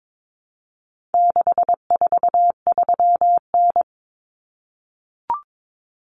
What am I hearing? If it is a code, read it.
Morse code, heard as a keyed tone, about 22 wpm, 700 Hz. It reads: 643N
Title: Common Cut Numbers - 4-Digit 22wpm